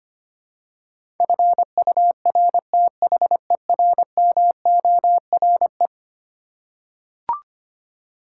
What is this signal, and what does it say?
Morse code, heard as a keyed tone, about 25 wpm, 700 Hz. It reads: FURTHERMORE
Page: morse-code-ninja-practice-25wpm